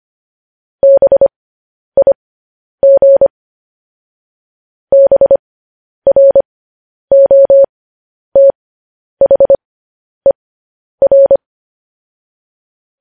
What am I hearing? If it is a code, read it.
Morse code, heard as a keyed tone, about 25 wpm, spaced as for 10 wpm, 550 Hz. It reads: BIG BROTHER
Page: News Headlines in Morse Code at 10 WPM